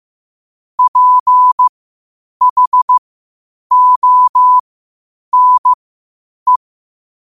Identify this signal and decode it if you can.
Morse code, heard as a keyed tone, about 15 wpm, spaced as for 8 wpm, 1000 Hz. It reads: PHONE
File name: qotdfiles / 8wpm